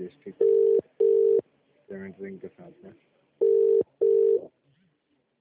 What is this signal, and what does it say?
Telephone call progress tone (UK), heard as a telephone network's tone, about 400, 450 Hz, with speech. Ringback tone.